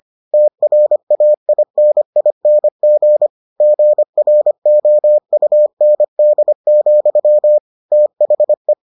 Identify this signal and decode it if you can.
Morse code, heard as a keyed tone, about 25 wpm, 600 Hz. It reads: TRAINING GROUND, THE